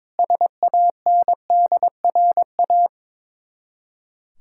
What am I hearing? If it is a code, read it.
Morse code, heard as a keyed tone, about 22 wpm, 700 Hz. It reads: SANDRA
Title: Practice QSO - Exchange 1 - Beginning - 22wpm